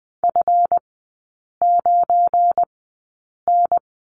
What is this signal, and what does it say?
Morse code, heard as a keyed tone, about 20 wpm, spaced as for 8 wpm, 700 Hz. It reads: F9N
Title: 218 Lesson 35e - Number 0 - Three Characters 20wpm